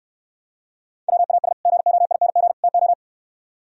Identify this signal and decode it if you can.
Morse code, heard as a keyed tone, about 67 wpm, 700 Hz. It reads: WAS GOING TO